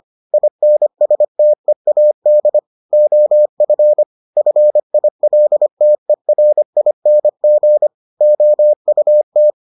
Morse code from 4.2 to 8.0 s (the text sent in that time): FILTERING